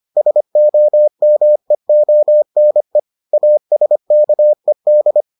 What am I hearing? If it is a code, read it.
Morse code, heard as a keyed tone, about 25 wpm, 600 Hz. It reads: SOMEONE ASKED